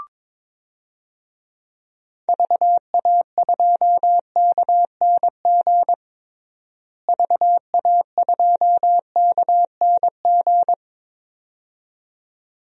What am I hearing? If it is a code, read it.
Morse code, heard as a keyed tone, about 22 wpm, 700 Hz. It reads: VA2KNG VA2KNG
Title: DXCC-100 Call Signs - 6 Characters - Repeated 2x - 22wpm